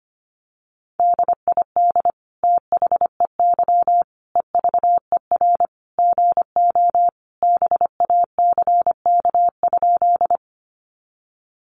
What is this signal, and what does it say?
Morse code, heard as a keyed tone, about 25 wpm, 700 Hz. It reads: DID THEY EVER GO BACK?
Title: Sentences from Top 400 Words 25wpm